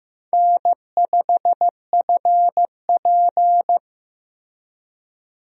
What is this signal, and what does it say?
Morse code, heard as a keyed tone, about 15 wpm, 700 Hz. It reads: N5FP
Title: USA Call Signs - 4 Characters - Repeated 2x - 15wpm